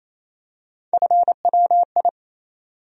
Morse code, heard as a keyed tone, about 28 wpm, 700 Hz. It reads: FWI
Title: USA Call Sign Postfixes - 3 Letters 28wpm